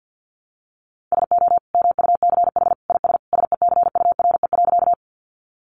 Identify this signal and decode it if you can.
Morse code, heard as a keyed tone, about 49 wpm, 700 Hz. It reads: SO MUCH IS SECURE.
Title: Sentences from Top 1200 Words 50wpm